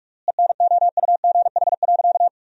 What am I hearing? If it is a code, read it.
Morse code, heard as a keyed tone, about 46 wpm, 700 Hz. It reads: ENOUGH.
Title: 3x Word-Spacing - Sentences from Top 400 Words - 45wpm